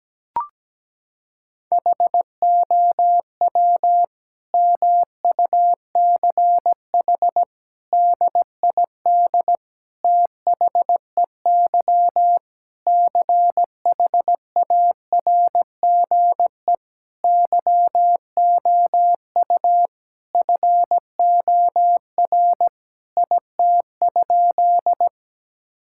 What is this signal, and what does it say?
Morse code, heard as a keyed tone, about 17 wpm, 700 Hz. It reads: HOW MUCH DID THEY CHARGE YOU FOR IT?
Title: Sentences from Top 1000 Words - Review Part 6 - 17wpm